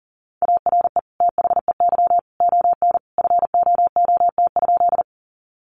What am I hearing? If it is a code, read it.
Morse code, heard as a keyed tone, about 40 wpm, 700 Hz. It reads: ARE THEY ON FOOT?
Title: Sentences from Top 500 Words 40wpm